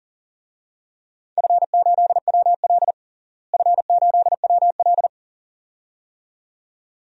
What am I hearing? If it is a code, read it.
Morse code, heard as a keyed tone, about 37 wpm, 700 Hz. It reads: F8WL F8WL
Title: DXCC-100 Call Signs - 4 Characters - Repeated 2x - 40wpm